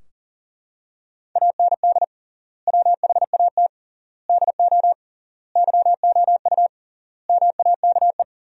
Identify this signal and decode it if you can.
Morse code, heard as a keyed tone, about 40 wpm, 700 Hz. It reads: AND WHAT DO YOU MAKE
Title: 3x Word-Spacing - Sentences from Top 100 Words - 40wpm